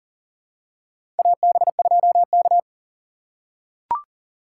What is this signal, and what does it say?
Morse code, heard as a keyed tone, about 39 wpm, 700 Hz. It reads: AB2K
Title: USA Call Signs - 4 Characters - 2x Character Spacing - 40wpm